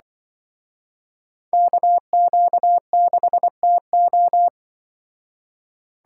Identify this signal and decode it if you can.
Morse code, heard as a keyed tone, about 24 wpm, 700 Hz. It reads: KQ6TO
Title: Speed Racing - USA 2x2 Call Signs 20wpm